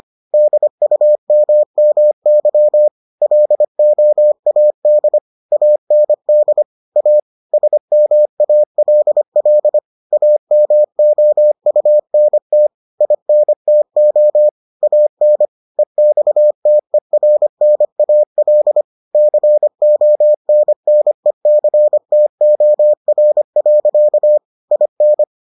Morse code, heard as a keyed tone, about 25 wpm, 600 Hz. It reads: DUMMY LOAD AND A SMALL AMOUNT INTO AN EXTERNAL CONNECTOR. IN